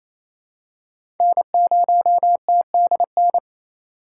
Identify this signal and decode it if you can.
Morse code, heard as a keyed tone, about 28 wpm, 700 Hz. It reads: N0TDN
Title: USA Call Signs - 8x Character Spacing - 28wpm